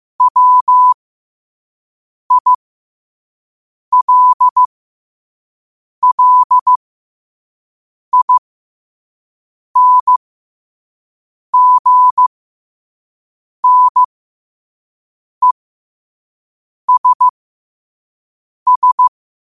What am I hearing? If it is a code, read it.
Morse code, heard as a keyed tone, about 15 wpm, spaced as for 5 wpm, 1000 Hz. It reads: WILLINGNESS